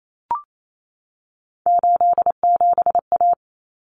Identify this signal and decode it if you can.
Morse code, heard as a keyed tone, about 28 wpm, 700 Hz. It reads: 87A
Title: Common Cut Numbers - 3-Digit 28wpm